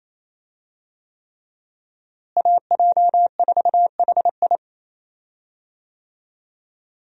Morse code, heard as a keyed tone, about 27 wpm, 700 Hz. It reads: AJ4HI
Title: USA Call Signs - 5 Characters - Repeated 3x - 28wpm